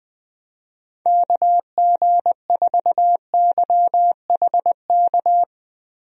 Morse code, heard as a keyed tone, about 20 wpm, 700 Hz. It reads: KG4YHK
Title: USA Call Signs - 6 Characters - 20wpm